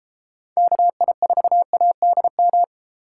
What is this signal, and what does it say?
Morse code, heard as a keyed tone, about 31 wpm, 700 Hz. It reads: KI4ADM